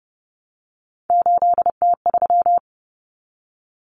Morse code, heard as a keyed tone, about 30 wpm, 700 Hz. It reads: 8T3
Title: Common Cut Numbers - 3-Digit 30wpm